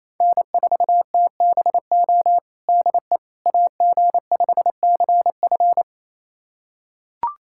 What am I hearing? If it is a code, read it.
Morse code, heard as a keyed tone, about 28 wpm, 700 Hz. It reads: N4TBO DE AG5CF
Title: USA Callsign DE Callsign - 5 Characters - 28wpm